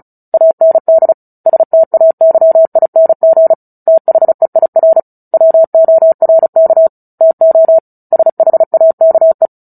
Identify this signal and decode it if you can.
Morse code, heard as a keyed tone, about 34 wpm, 650 Hz. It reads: AND STAYING THEIR WORK TO SHAKE
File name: Book Two / 09-Chapter 9 - 35 WPM